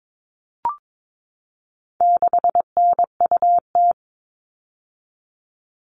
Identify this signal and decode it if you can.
Morse code, heard as a keyed tone, about 22 wpm, 700 Hz. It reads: 6NUT
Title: All Cut Numbers - 4-Digit 22wpm